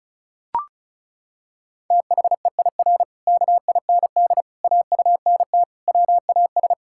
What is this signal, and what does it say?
Morse code, heard as a keyed tone, about 34 wpm, 700 Hz. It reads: THEIR KIND AUNT WAS
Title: Pangram Challenge v2 - 35wpm